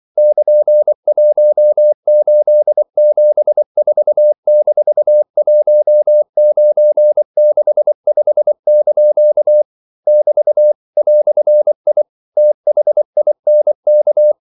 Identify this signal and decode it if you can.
Morse code, heard as a keyed tone, about 24 wpm, 600 Hz. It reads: <KN>1874-1965) <BT> "I THINK
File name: CW / Practice / EnglishQuotations0424